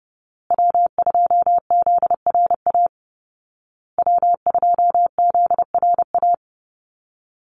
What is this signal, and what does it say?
Morse code, heard as a keyed tone, about 28 wpm, 700 Hz. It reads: W2ZRA W2ZRA
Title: Call Signs Repeated 2x - 30wpm